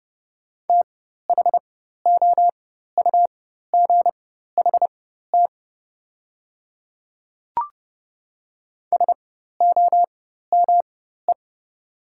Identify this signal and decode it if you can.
Morse code, heard as a keyed tone, about 30 wpm, spaced as for 14 wpm, 700 Hz. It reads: THOUGHT SOME
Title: 4x Character Spacing - Top 300 Words - 30wpm